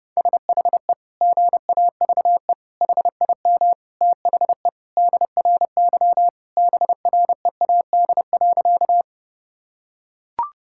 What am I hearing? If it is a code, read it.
Morse code, heard as a keyed tone, about 32 wpm, 700 Hz. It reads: SHE GAVE HIM THE DRY BREAD.
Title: Sentences from Top 1600 Words 30wpm